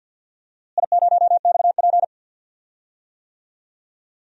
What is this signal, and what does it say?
Morse code, heard as a keyed tone, about 50 wpm, 700 Hz. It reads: I0XP
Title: DXCC-100 Call Signs - 4 Characters - Repeated 3x - 50wpm